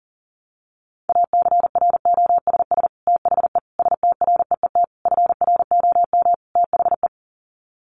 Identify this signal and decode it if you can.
Morse code, heard as a keyed tone, about 40 wpm, 700 Hz. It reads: ACROSS THE STREET FROM THE
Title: Sets of 5 Words Encore 40wpm